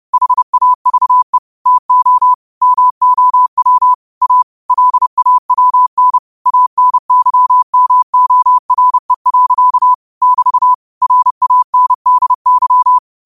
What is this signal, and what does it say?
Morse code, heard as a keyed tone, about 30 wpm, 1000 Hz. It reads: HAVE TO MOW A LAWN ANYMORE. <BT> RANDY
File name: qotdfiles / 30wpm